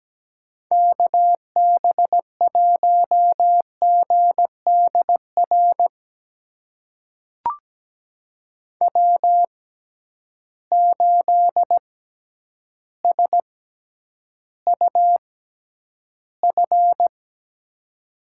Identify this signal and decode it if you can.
Morse code, heard as a keyed tone, about 17 wpm, spaced as for 9 wpm, 700 Hz. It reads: KB1GDR W8SUF
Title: USA Call Signs - 6x Character Spacing - 17wpm